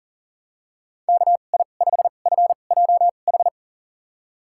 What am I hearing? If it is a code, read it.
Morse code, heard as a keyed tone, about 40 wpm, spaced as for 29 wpm, 700 Hz. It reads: KI5FJH
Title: USA Call Signs - 6 Characters - 2x Character Spacing - 40wpm